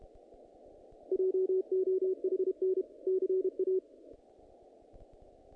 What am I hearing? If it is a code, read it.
Morse code, heard as a keyed tone, about 31 wpm, 380 Hz. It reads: JOHN CA
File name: contests / naqp-cw-k5tr-op-k5pi- -2000